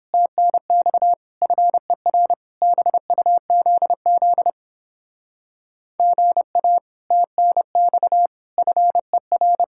Morse code, heard as a keyed tone, about 29 wpm, 700 Hz. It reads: TNX FER BUZZ GA TNX FER